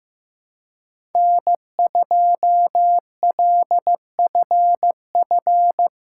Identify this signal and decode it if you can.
Morse code, heard as a keyed tone, about 16 wpm, 700 Hz. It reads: N2LFF
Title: USA Callsign DE Callsign - 5 Characters - 15wpm